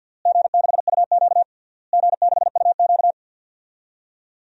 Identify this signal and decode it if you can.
Morse code, heard as a keyed tone, about 52 wpm, 700 Hz. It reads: G6UQ G6UQ
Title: DXCC-100 Call Signs - 4 Characters - Repeated 3x - 50wpm